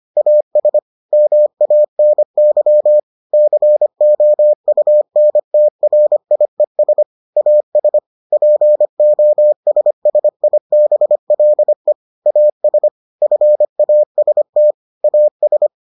Morse code, heard as a keyed tone, about 25 wpm, 600 Hz. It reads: AS MANY COUNTRIES AS POSSIBLE AS FAST AS